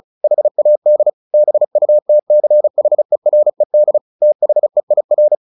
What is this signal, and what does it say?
Morse code, heard as a keyed tone, about 34 wpm, 600 Hz. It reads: HAD BUTCHERED THEIR
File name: the old man and the sea / the old man and the sea 1 35wpm